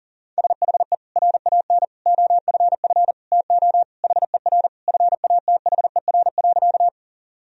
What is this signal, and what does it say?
Morse code, heard as a keyed tone, about 40 wpm, 700 Hz. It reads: SHE RAN OFF TO HER FATHER.
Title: Sentences from Top 1000 Words - Review Part 3 - 40wpm